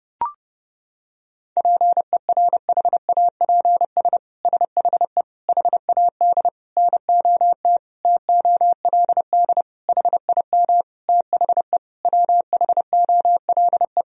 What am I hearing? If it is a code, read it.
Morse code, heard as a keyed tone, about 32 wpm, 700 Hz. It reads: PERHAPS SHE HAD NOT TOLD HIM THE WHOLE
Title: Sentences from Top 1000 Words - Review Part 1 - 30wpm